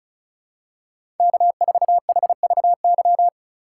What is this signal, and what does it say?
Morse code, heard as a keyed tone, about 36 wpm, 700 Hz. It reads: K4HVY
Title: USA Call Signs - 5 Characters - 6x Character Spacing - 35wpm